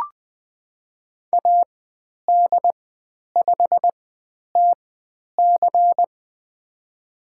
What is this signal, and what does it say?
Morse code, heard as a keyed tone, about 20 wpm, spaced as for 10 wpm, 700 Hz. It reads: AD5TC
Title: 095 Lesson 16g - Letter c - Call Signs 20wpm